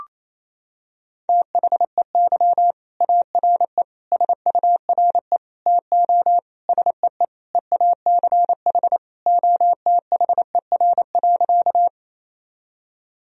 Morse code, heard as a keyed tone, about 27 wpm, 700 Hz. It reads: THEY ARE SURE TO SEE EACH OTHER.
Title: Sentences from Top 1000 Words - Review Part 1 - 28wpm